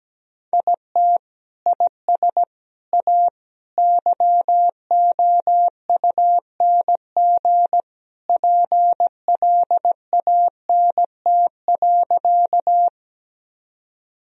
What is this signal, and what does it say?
Morse code, heard as a keyed tone, about 17 wpm, 700 Hz. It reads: IT IS A YOUNG PLANT.
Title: Sentences from Top 1000 Words - Review Part 1 - 17wpm